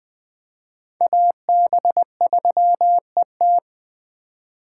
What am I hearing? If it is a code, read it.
Morse code, heard as a keyed tone, about 20 wpm, 700 Hz. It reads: AB3ET
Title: USA Call Signs - 5 Characters - 2x Character Spacing - 20wpm